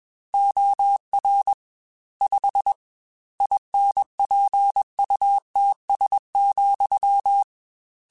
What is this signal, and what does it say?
Morse code, heard as a keyed tone, about 21 wpm, 800 Hz. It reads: OR 5 INPUTS,
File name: Technical Subjects / Technical Subjects-mp3s / AND Gate @20